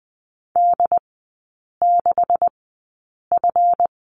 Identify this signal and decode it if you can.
Morse code, heard as a keyed tone, about 20 wpm, spaced as for 8 wpm, 700 Hz. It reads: D6F
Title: 138 Lesson 23e - Letter f - Three Characters 20wpm